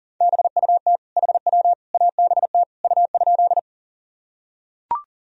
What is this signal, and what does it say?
Morse code, heard as a keyed tone, about 37 wpm, 700 Hz. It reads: BUT HW ABT U?
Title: Sentences from Top 100 Words and Abbreviations 40wpm